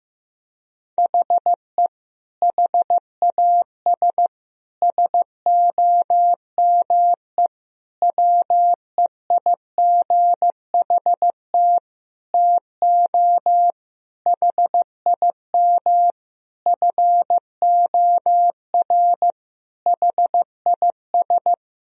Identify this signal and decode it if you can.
Morse code, heard as a keyed tone, about 15 wpm, 700 Hz. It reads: HE HAS SOME WEIGHT TO HIM FOR HIS
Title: Sentences from Top 1000 Words - Review Part 3 - 15wpm